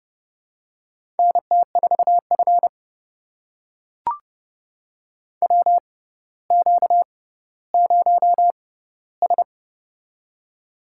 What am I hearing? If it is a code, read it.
Morse code, heard as a keyed tone, about 31 wpm, spaced as for 16 wpm, 700 Hz. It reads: NT4F WQ0S
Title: USA Call Signs - 4 Characters - 6x Character Spacing - 30wpm